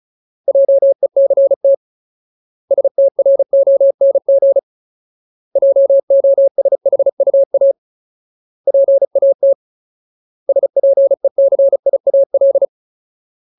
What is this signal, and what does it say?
Morse code, heard as a keyed tone, about 34 wpm, 550 Hz. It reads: JECT STRONG JOSHUA PAT SPECIAL
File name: Ultimate Word List / 35WPM Ultimate